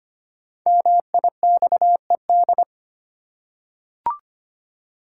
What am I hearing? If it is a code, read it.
Morse code, heard as a keyed tone, about 25 wpm, 700 Hz. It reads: MIXED
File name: 5 Letter Words - First 3 Letters Given - 25wpm